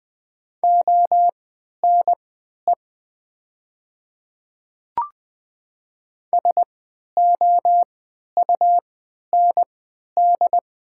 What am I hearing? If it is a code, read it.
Morse code, heard as a keyed tone, about 20 wpm, spaced as for 11 wpm, 700 Hz. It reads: ONE SOUND
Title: 3x Character Spacing - Top 100 Words - 20wpm